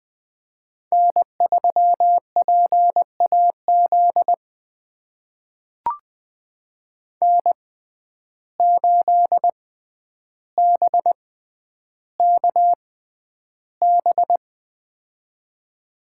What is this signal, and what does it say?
Morse code, heard as a keyed tone, about 20 wpm, spaced as for 10 wpm, 700 Hz. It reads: N3PAZ N8BKB
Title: USA Call Signs - 5 Characters - 6x Character Spacing - 20wpm